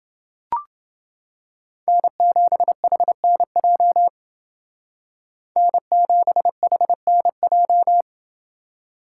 Morse code, heard as a keyed tone, about 29 wpm, 700 Hz. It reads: N7HNJ N7HNJ